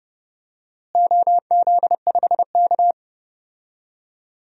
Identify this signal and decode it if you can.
Morse code, heard as a keyed tone, about 30 wpm, 700 Hz. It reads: OZ5K
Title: DXCC-100 Call Signs - 4 Characters - Repeated 2x - 30wpm